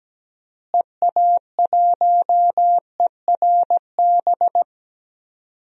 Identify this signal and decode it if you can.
Morse code, heard as a keyed tone, about 17 wpm, 700 Hz. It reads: EA1ERB
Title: DXCC-100 Call Signs - 6 Characters - Repeated 3x - 17wpm